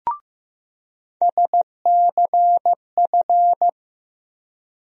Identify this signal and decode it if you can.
Morse code, heard as a keyed tone, about 15 wpm, 700 Hz. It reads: SCF
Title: USA Call Sign Postfixes - 3 Letters 15wpm